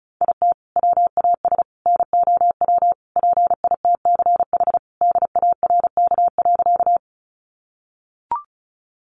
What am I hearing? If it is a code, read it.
Morse code, heard as a keyed tone, about 34 wpm, 700 Hz. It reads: IT WAS NOW PITCH DARK.
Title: Sentences from Top 1300 Words 35wpm